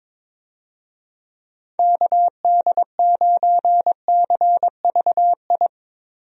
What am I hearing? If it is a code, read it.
Morse code, heard as a keyed tone, about 22 wpm, 700 Hz. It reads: KD9CVI
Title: USA Call Signs - 6 Characters - Repeated 3x - 22wpm